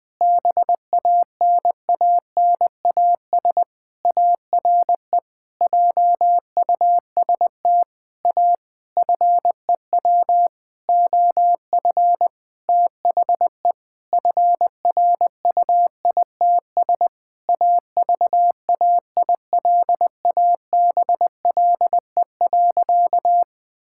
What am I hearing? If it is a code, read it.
Morse code, heard as a keyed tone, about 20 wpm, 700 Hz. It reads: BANANAS ARE JUST A FEW OF THE FRUITS AVAILABLE.
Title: Sentences with 6-Word Phrases - 20wpm